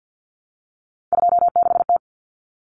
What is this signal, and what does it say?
Morse code, heard as a keyed tone, about 50 wpm, 700 Hz. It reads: 26T
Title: Common Cut Numbers - 3-Digit 50wpm